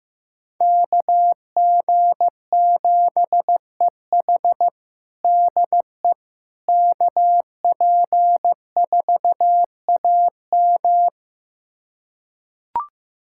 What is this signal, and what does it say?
Morse code, heard as a keyed tone, about 16 wpm, 700 Hz. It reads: KG7EH DE KP4AM
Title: USA Callsign DE Callsign - 5 Characters - 15wpm